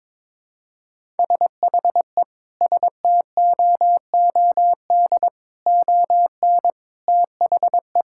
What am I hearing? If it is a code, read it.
Morse code, heard as a keyed tone, about 22 wpm, 700 Hz. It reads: SHE STOOD ON THE
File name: practice / 22wpm / Sentences with 4-Word Phrases - 22wpm